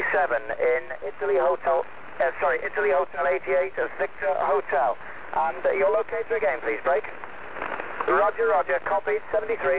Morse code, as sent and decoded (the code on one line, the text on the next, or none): none